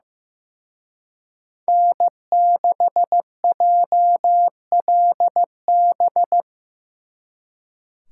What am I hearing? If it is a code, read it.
Morse code, heard as a keyed tone, about 18 wpm, 700 Hz. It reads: N6JLB
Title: Speed Racing - USA 1x3 Call Signs 15wpm